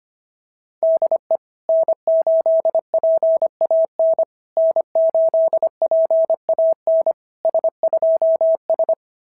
Morse code, heard as a keyed tone, about 24 wpm, 650 Hz. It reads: DE N8PAN N8PAN S2S
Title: SOTA QSO Practice 25wpm